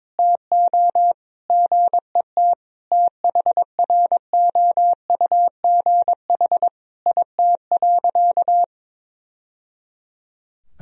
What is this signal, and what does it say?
Morse code, heard as a keyed tone, about 22 wpm, 700 Hz. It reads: TO GET THROUGH IT.
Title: Sentences from Top 1000 Words - Review Part 2 - 22wpm